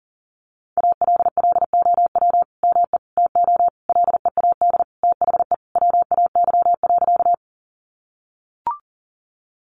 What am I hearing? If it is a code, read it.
Morse code, heard as a keyed tone, about 40 wpm, 700 Hz. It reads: ALLOW ME TO LEAD THE WAY.
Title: Sentences from Top 700 Words 40wpm